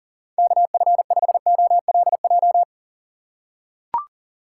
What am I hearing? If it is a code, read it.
Morse code, heard as a keyed tone, about 37 wpm, 700 Hz. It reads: KF5OLJ
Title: USA Call Signs - 6 Characters - 2x Character Spacing - 40wpm